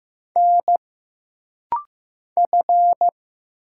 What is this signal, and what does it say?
Morse code, heard as a keyed tone, about 15 wpm, 700 Hz. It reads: N F
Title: Single Letter-Number Alt - Rapid-Fire 15wpm